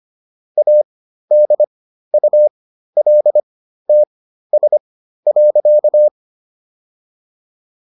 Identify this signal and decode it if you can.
Morse code, heard as a keyed tone, about 25 wpm, spaced as for 13 wpm, 600 Hz. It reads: ADULTS.